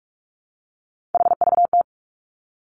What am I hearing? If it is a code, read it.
Morse code, heard as a keyed tone, about 46 wpm, 700 Hz. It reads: HVT
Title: Three Challenging Letter-Number Combinations 45wpm